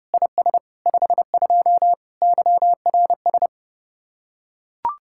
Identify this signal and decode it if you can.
Morse code, heard as a keyed tone, about 31 wpm, 700 Hz. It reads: IS 52 YRS